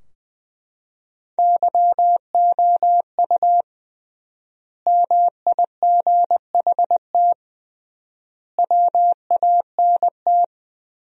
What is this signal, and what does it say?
Morse code, heard as a keyed tone, about 20 wpm, 700 Hz. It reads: YOU MIGHT WANT
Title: 3x Word-Spacing - Sentences from Top 300 Words - 20wpm